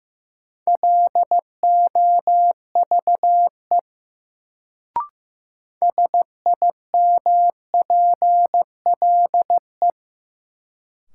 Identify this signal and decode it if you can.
Morse code, heard as a keyed tone, about 15 wpm, 700 Hz. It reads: LOVE SIMPLE
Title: Sentences from Top 400 Words - New Word Drill - 15wpm